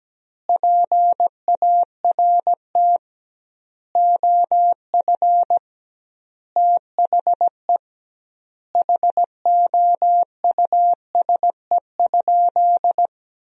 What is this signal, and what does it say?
Morse code, heard as a keyed tone, about 17 wpm, 700 Hz. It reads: PART OF THE HOUSE?
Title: 2x Word-Spacing - Sentences from Top 200 Words - 17wpm